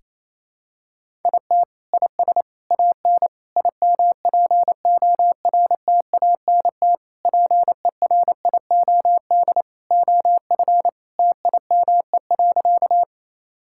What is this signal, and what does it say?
Morse code, heard as a keyed tone, about 28 wpm, 700 Hz. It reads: IT IS AN IMPORTANT PERIOD OF TIME.
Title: Sentences from Top 1000 Words - Review Part 5 - 28wpm